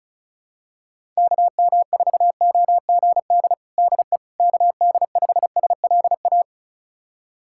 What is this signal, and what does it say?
Morse code, heard as a keyed tone, about 34 wpm, 700 Hz. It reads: KM4OGD DE KD5SLA
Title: USA Callsign DE Callsign - 6 Characters - 35wpm